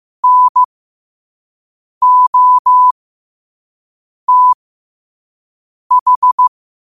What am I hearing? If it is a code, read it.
Morse code, heard as a keyed tone, about 15 wpm, spaced as for 5 wpm, 1000 Hz. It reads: NOTH
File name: qotdfiles / 5wpm